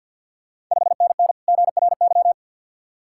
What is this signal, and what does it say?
Morse code, heard as a keyed tone, about 51 wpm, 690 Hz. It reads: 5NN GRY
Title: Ontario QSO Party Exchange 50wpm